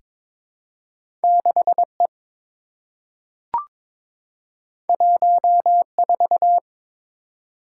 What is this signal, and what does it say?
Morse code, heard as a keyed tone, about 22 wpm, 700 Hz. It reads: 6E 14
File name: practice / 22wpm / Two Letter-Number Combinations - 22wpm